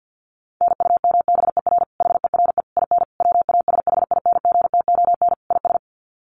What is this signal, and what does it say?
Morse code, heard as a keyed tone, about 48 wpm, 700 Hz. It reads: NUMBER HERE IN WASHINGTON IS
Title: Sets of 5 Words Encore2 50wpm